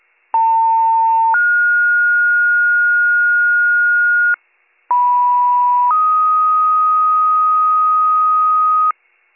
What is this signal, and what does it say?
Two-tone page, heard as a one-second tone, then a three-second tone, about 900 then 1430 Hz, about 980 then 1250 Hz, with noise. It is heard from 0.3 to 8.9 s.